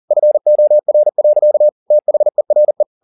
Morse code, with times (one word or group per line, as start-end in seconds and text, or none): 0.1-1.7 FOR.
1.9-2.8 THERE